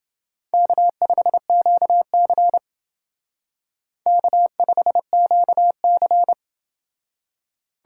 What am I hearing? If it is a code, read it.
Morse code, heard as a keyed tone, about 29 wpm, 700 Hz. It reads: K5QC K5QC